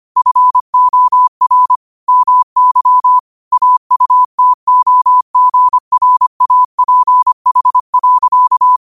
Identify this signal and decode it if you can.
Morse code, heard as a keyed tone, about 25 wpm, 1000 Hz. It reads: FOR MY AUTOGRAPH.
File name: qotdfiles / 25wpm